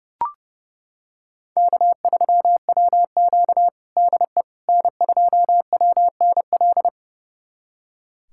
Morse code, heard as a keyed tone, about 30 wpm, 700 Hz. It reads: K3WQ DE N2WNL
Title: USA Callsign DE Callsign 30wpm